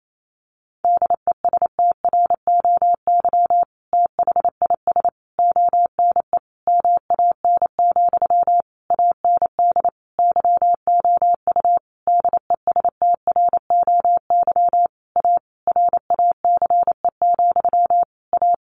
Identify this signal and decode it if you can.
Morse code, heard as a keyed tone, about 28 wpm, 700 Hz. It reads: DESTROY THIS ONE MAN, AND YOU DESTROY A RACE, A
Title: A Sound Of Thunder 28wpm